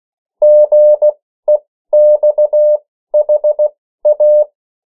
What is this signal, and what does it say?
Morse code, heard as a keyed tone, about 16 wpm, spaced as for 13 wpm, 600 Hz. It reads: GEXHA
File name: morse mp3 / 14 urok 13